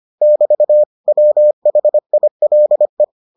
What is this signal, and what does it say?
Morse code, heard as a keyed tone, about 25 wpm, 600 Hz. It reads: <BT> WHILE